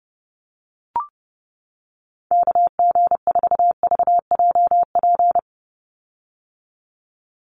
Morse code, heard as a keyed tone, about 30 wpm, 700 Hz. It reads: KG4VJP
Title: USA Call Signs 30wpm